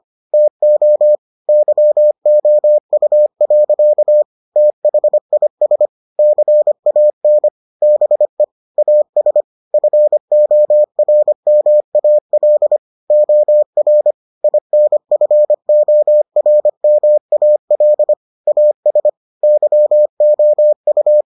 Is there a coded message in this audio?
Yes: Morse code from 0.3 to 21.2 s, heard as a keyed tone, about 25 wpm, 600 Hz. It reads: TO YOU. THIS CAN BE AS FORMAL OR INFORMAL AS YOU